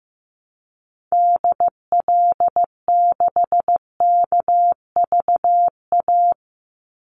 Morse code, heard as a keyed tone, about 15 wpm, 700 Hz. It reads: DL6KVA
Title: Call Signs Repeated 2x - 15wpm